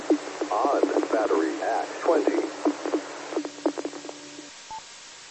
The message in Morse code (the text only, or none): none